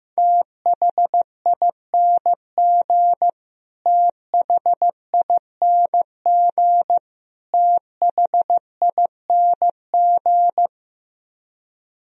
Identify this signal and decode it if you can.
Morse code, heard as a keyed tone, about 16 wpm, 700 Hz. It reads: THING THING THING
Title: Top 100 Words Repeated 15wpm